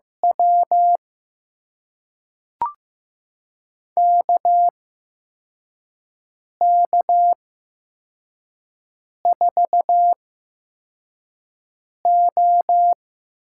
Morse code, heard as a keyed tone, about 15 wpm, spaced as for 4 wpm, 700 Hz. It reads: W KK4O
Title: USA Call Signs - 8x Character Spacing - 15wpm